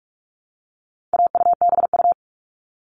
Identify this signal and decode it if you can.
Morse code, heard as a keyed tone, about 44 wpm, 700 Hz. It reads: AUBU
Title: All Cut Numbers - 4-Digit 45wpm